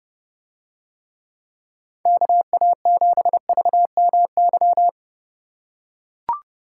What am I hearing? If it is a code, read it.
Morse code, heard as a keyed tone, about 28 wpm, 700 Hz. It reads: KA7VMY